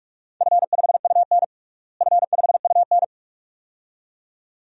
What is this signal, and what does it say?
Morse code, heard as a keyed tone, about 42 wpm, 700 Hz. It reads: F5UN F5UN
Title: DXCC-100 Call Signs - 4 Characters - Repeated 2x - 45wpm